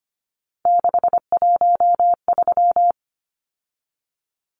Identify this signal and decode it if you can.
Morse code, heard as a keyed tone, about 25 wpm, 700 Hz. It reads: 613
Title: Numbers - 3-Digit 25wpm